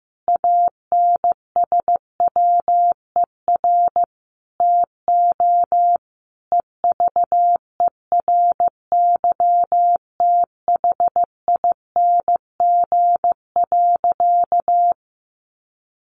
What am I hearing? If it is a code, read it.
Morse code, heard as a keyed tone, about 15 wpm, 700 Hz. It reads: ANSWER TO EVERYTHING.
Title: Sentences from Top 700 Words 15wpm